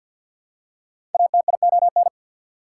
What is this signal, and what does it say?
Morse code, heard as a keyed tone, about 54 wpm, 690 Hz. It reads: ATION
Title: Prefixes and Suffixes 50wpm